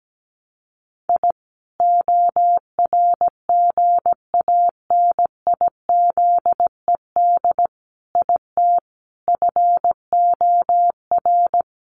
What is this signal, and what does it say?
Morse code, heard as a keyed tone, about 17 wpm, 700 Hz. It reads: I ORGANIZED IT FOR
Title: Sentences from Top 1300 Words 17wpm